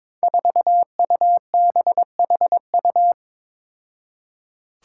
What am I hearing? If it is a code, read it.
Morse code, heard as a keyed tone, about 22 wpm, 700 Hz. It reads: 4UBHU
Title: Five Challenging Letter-Number Combinations 22wpm